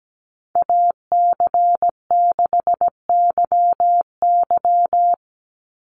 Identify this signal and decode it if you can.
Morse code, heard as a keyed tone, about 17 wpm, 700 Hz. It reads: AC6YY
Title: Call Signs Repeated 2x - 17wpm